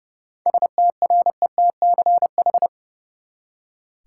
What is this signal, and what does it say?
Morse code, heard as a keyed tone, about 30 wpm, 700 Hz. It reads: STRETCH
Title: Seven Letter Words - 30wpm